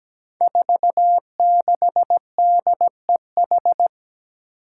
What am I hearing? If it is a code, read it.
Morse code, heard as a keyed tone, about 17 wpm, 700 Hz. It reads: 46DEH
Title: Five Challenging Letter-Number Combinations 17wpm